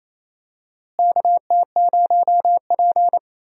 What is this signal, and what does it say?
Morse code, heard as a keyed tone, about 28 wpm, 700 Hz. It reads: KT0P
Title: USA Call Signs - 4 Characters - Repeated 3x - 28wpm